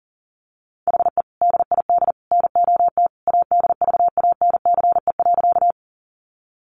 Morse code, heard as a keyed tone, about 40 wpm, 700 Hz. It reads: HE DID NOT ADVANCE.
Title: Sentences from Top 1400 Words 40wpm